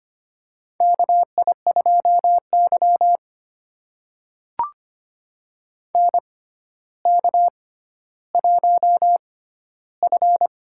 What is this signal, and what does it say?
Morse code, heard as a keyed tone, about 25 wpm, spaced as for 13 wpm, 700 Hz. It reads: KI2Y NK1F
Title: USA Call Signs - 4 Characters - 6x Character Spacing - 25wpm